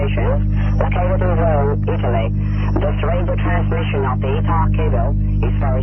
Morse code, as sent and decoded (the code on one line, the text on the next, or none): none